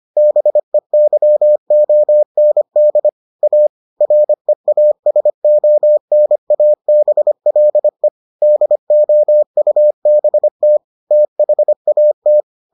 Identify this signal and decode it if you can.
Morse code, heard as a keyed tone, about 25 wpm, 600 Hz. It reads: BEYOND A REASONABLE DOUBT THAT